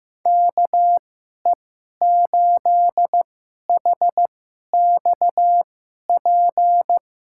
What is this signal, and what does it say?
Morse code, heard as a keyed tone, about 15 wpm, spaced as for 11 wpm, 700 Hz. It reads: KE8HXP
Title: USA Call Signs - 2x Character Spacing - 15wpm